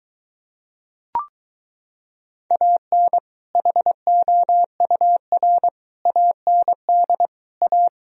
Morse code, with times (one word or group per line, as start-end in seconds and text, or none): 2.5-3.2 AN
3.6-5.7 HOUR
6.1-7.3 AND
7.6-7.9 A